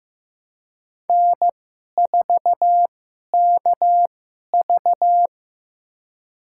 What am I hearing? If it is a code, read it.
Morse code, heard as a keyed tone, about 15 wpm, spaced as for 11 wpm, 700 Hz. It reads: N4KV
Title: USA Call Signs - 4 Characters - 2x Character Spacing - 15wpm